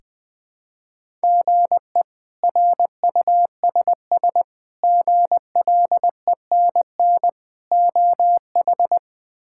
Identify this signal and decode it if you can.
Morse code, heard as a keyed tone, about 21 wpm, 700 Hz. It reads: GE RUSS GLENN OH
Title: K1USN SST QSO Practice - 20wpm